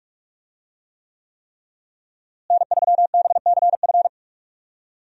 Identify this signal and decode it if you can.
Morse code, heard as a keyed tone, about 46 wpm, 700 Hz. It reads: N3BCF
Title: USA Call Signs - 5 Characters - Repeated 3x - 45wpm